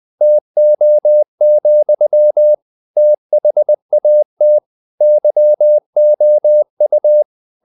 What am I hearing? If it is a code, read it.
Morse code, heard as a keyed tone, about 20 wpm, 600 Hz. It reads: TO, THAT YOU
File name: the old man and the sea / the old man and the sea part 1 20wpm